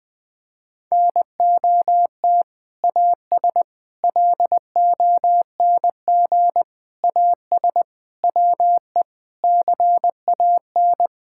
Morse code, heard as a keyed tone, about 20 wpm, 700 Hz. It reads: NOT AS LONG AS WE CAN
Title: Sentences from Top 1000 Words - Review Part 6 - 20wpm